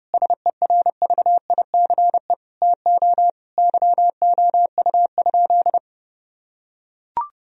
Morse code, heard as a keyed tone, about 30 wpm, 700 Hz. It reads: SERVICE TO YOU?
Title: Sentences from Top 1000 Words - Review Part 4 - 30wpm